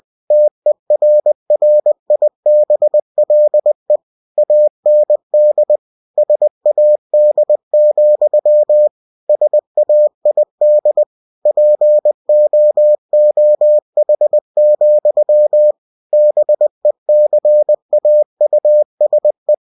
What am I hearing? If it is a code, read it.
Morse code, heard as a keyed tone, about 20 wpm, 600 Hz. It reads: TERRIBLE AND SAD, SAID POOH, BECAUSE
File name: cw-books / winnie-the-pooh / winnie-20wpm-page-0029